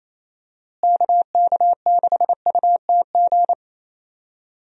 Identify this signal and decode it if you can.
Morse code, heard as a keyed tone, about 28 wpm, 700 Hz. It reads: KK6UTG
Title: USA Call Signs - 6 Characters - Repeated 3x - 28wpm